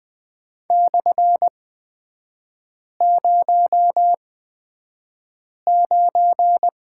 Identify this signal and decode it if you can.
Morse code, heard as a keyed tone, about 20 wpm, spaced as for 5 wpm, 700 Hz. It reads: /09
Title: 243 Lesson 39e - Letter z - Three Characters 20wpm